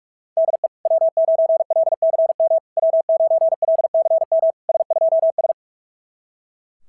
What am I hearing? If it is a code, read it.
Morse code, heard as a keyed tone, about 44 wpm, 650 Hz. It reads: DE W9LCM W9LCM S2S
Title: SOTA QSO Practice 45wpm